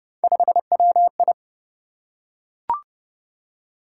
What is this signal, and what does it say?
Morse code, heard as a keyed tone, about 29 wpm, 700 Hz. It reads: 5WI